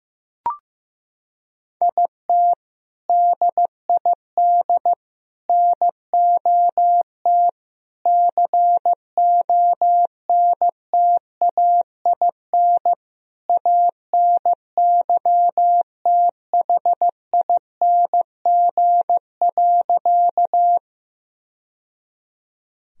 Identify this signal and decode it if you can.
Morse code, heard as a keyed tone, about 15 wpm, 700 Hz. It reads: IT DID NOT CONTAIN ANYTHING.
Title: Sentences from Top 1000 Words - Review Part 2 - 15wpm